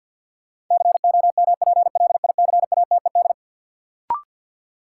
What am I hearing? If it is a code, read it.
Morse code, heard as a keyed tone, about 52 wpm, 700 Hz. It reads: COMPLICATED
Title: Sentences from Top 1200 Words - New Word Drill Alt - 50wpm